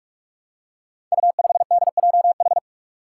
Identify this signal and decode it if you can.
Morse code, heard as a keyed tone, about 41 wpm, 700 Hz. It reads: U5DJH
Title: Five Challenging Letter-Number Combinations 45wpm